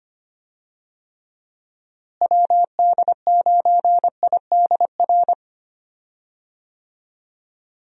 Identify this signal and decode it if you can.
Morse code, heard as a keyed tone, about 25 wpm, 700 Hz. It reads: WD9IDR
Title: USA Call Signs - 6 Characters - Repeated 2x - 25wpm